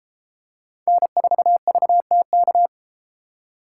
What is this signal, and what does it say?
Morse code, heard as a keyed tone, about 32 wpm, 700 Hz. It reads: N4VTK